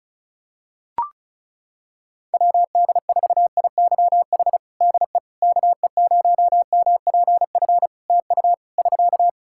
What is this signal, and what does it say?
Morse code, heard as a keyed tone, about 35 wpm, 700 Hz. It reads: WD4IYH DE KE0MPF TU <SK>
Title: Practice QSO - Exchange 4 - Ending - 35wpm